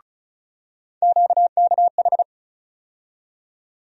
Morse code, heard as a keyed tone, about 36 wpm, 700 Hz. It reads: QKH
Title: USA Call Sign Postfixes - 3 Letters 35wpm